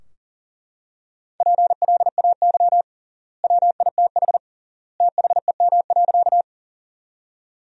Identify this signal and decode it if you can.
Morse code, heard as a keyed tone, about 40 wpm, 700 Hz. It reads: PLAY WITH THEM.
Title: 3x Word-Spacing - Sentences from Top 200 Words - 40wpm